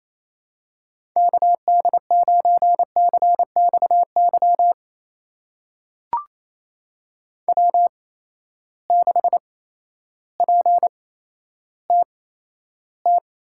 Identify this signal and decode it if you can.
Morse code, heard as a keyed tone, about 28 wpm, spaced as for 13 wpm, 700 Hz. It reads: KD9CXY W6PTT
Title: USA Call Signs - 8x Character Spacing - 28wpm